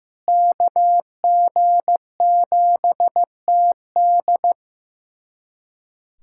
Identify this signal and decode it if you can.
Morse code, heard as a keyed tone, about 15 wpm, 700 Hz. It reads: KG7TD
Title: USA Callsign DE Callsign - 5 Characters - 15wpm